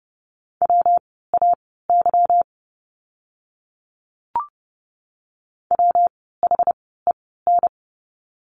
Morse code, heard as a keyed tone, about 30 wpm, spaced as for 17 wpm, 700 Hz. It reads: WAY WHEN
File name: course / 30:10wpm / 237 Lesson 38f - Letter q - Words 30wpm